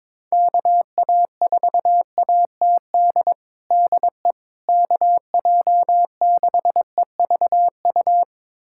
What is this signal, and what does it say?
Morse code, heard as a keyed tone, about 22 wpm, 700 Hz. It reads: KA4ATD DE KJ6EVU
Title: USA Callsign DE Callsign - 6 Characters - 22wpm